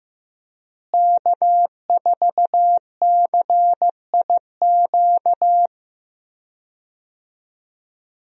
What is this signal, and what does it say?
Morse code, heard as a keyed tone, about 15 wpm, 700 Hz. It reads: K4CIQ
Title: USA Call Signs - 5 Characters - Repeated 3x - 15wpm